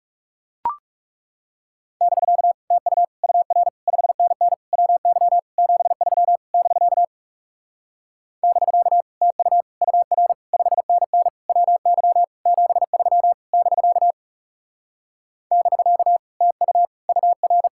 Morse code, heard as a keyed tone, about 40 wpm, 700 Hz. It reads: <BK> TU UR 5NN WY 73 <BK> <BK> TU UR 5NN WY 73 <BK> <BK> TU UR